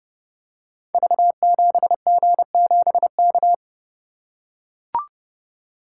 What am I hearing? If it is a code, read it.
Morse code, heard as a keyed tone, about 30 wpm, 700 Hz. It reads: V7G7K
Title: Code Groups - 5 Letter:Numbers - 30wpm